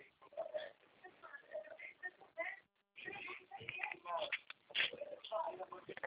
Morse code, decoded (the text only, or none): none